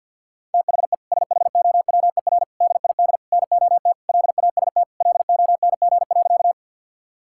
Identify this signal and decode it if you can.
Morse code, heard as a keyed tone, about 50 wpm, 700 Hz. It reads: THE SHOWER DID NOT LAST LONG.
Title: Sentences from Top 1600 Words 50wpm